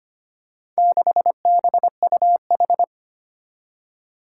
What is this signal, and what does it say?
Morse code, heard as a keyed tone, about 25 wpm, 700 Hz. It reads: 6BUH
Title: Four Challenging Letter-Number Combinations 25wpm